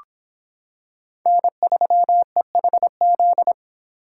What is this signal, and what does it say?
Morse code, heard as a keyed tone, about 26 wpm, 700 Hz. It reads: N3EHZ